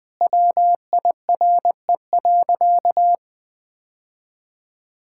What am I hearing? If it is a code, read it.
Morse code, heard as a keyed tone, about 20 wpm, 700 Hz. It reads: WIRE.
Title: Sentences with 3-Word Phrases - 20wpm